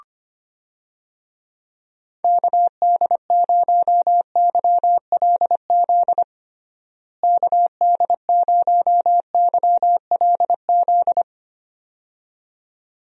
Morse code, heard as a keyed tone, about 25 wpm, 700 Hz. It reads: KD0YLZ KD0YLZ
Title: USA Call Signs - 6 Characters - Repeated 2x - 25wpm